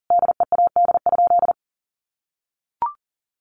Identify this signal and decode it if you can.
Morse code, heard as a keyed tone, about 40 wpm, 700 Hz. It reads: DEAD?
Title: Sentences from Top 1500 Words 40wpm